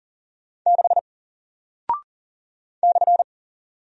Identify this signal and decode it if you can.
Morse code, heard as a keyed tone, about 40 wpm, 700 Hz. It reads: 6 /
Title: Single Character (Letter:Number:Punctuation:Prosign) Alt 40wpm